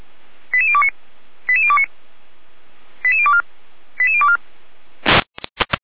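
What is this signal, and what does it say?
Five-tone selective call (ZVEI), heard as quick runs of five tones, tones of about 70 ms. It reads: 88029 88029 88024 88024
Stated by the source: ZVEI five-tone selective call 88169, received by SDR.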